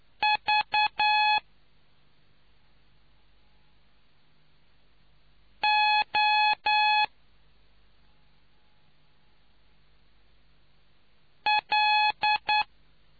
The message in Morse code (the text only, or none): V O L